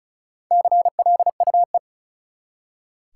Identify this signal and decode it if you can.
Morse code, heard as a keyed tone, about 34 wpm, 700 Hz. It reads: CLUE
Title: Four Letter Words - 35wpm